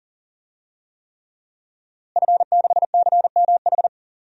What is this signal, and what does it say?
Morse code, heard as a keyed tone, about 37 wpm, 700 Hz. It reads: F6CMH
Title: DXCC-100 Call Signs - 5 Characters - Repeated 3x - 40wpm